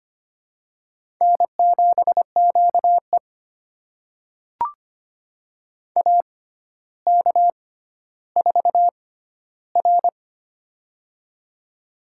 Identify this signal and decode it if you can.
Morse code, heard as a keyed tone, about 25 wpm, spaced as for 13 wpm, 700 Hz. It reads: N7QE AK4R
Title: USA Call Signs - 4 Characters - 6x Character Spacing - 25wpm